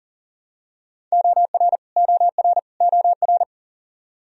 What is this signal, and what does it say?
Morse code, heard as a keyed tone, about 37 wpm, 700 Hz. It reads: OR OR OR
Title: Top 100 Words Repeated 40wpm